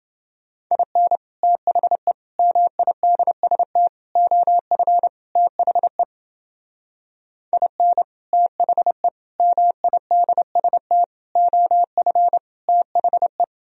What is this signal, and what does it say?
Morse code, heard as a keyed tone, about 30 wpm, 700 Hz. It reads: IN THE MIDST OF THE IN THE MIDST OF THE